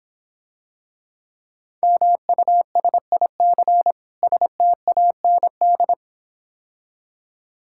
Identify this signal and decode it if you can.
Morse code, heard as a keyed tone, about 26 wpm, 700 Hz. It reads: MUSIC STAND